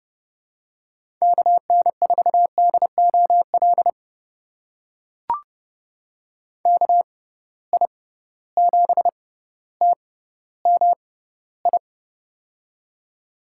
Morse code, heard as a keyed tone, about 30 wpm, spaced as for 15 wpm, 700 Hz. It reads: KN4DOL KI7TMI
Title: USA Call Signs - 6 Characters - 6x Character Spacing - 30wpm